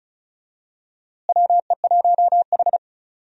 Morse code, heard as a keyed tone, about 36 wpm, 700 Hz. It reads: WE1H
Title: USA Call Signs - 4 Characters - Repeated 3x - 35wpm